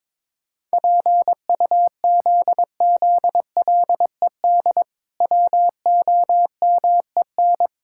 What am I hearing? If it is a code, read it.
Morse code, heard as a keyed tone, about 22 wpm, 700 Hz. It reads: PUZZLED WOMEN
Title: Pangram Challenge v2 - 22wpm